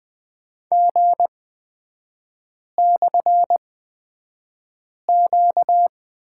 Morse code, heard as a keyed tone, about 20 wpm, spaced as for 5 wpm, 700 Hz. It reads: G/Q
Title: 236 Lesson 38e - Letter q - Three Characters 20wpm